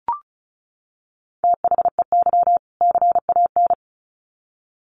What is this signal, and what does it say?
Morse code, heard as a keyed tone, about 34 wpm, 700 Hz. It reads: THEY CAN
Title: Sets of 2 Words 35wpm